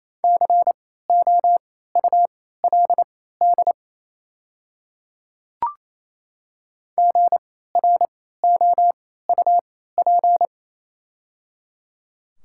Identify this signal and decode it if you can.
Morse code, heard as a keyed tone, about 28 wpm, spaced as for 16 wpm, 700 Hz. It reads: COULD GROUP
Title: 3x Character Spacing - Top 300 Words - 28wpm